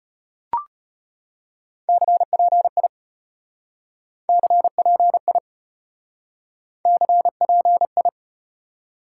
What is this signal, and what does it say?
Morse code, heard as a keyed tone, about 35 wpm, 700 Hz. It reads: CPI CPI CPI